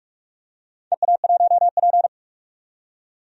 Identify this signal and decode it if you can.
Morse code, heard as a keyed tone, about 43 wpm, 700 Hz. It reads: EA1P
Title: DXCC-100 Call Signs - 4 Characters - 45wpm